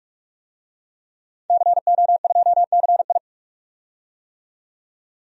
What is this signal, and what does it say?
Morse code, heard as a keyed tone, about 45 wpm, 700 Hz. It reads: CO2CI
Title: DXCC-100 Call Signs - 5 Characters - 45wpm